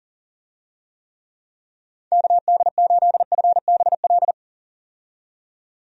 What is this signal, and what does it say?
Morse code, heard as a keyed tone, about 40 wpm, 700 Hz. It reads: KD8FBL
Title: USA Call Signs - 6 Characters - 40wpm